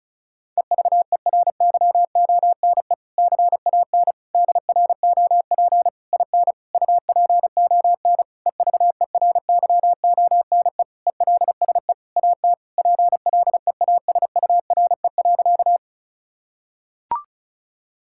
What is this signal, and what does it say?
Morse code, heard as a keyed tone, about 34 wpm, 700 Hz. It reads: EVERYONE CAN DROP IN UPON EVERYONE ELSE AT PLEASURE.
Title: Sentences from Top 1600 Words 35wpm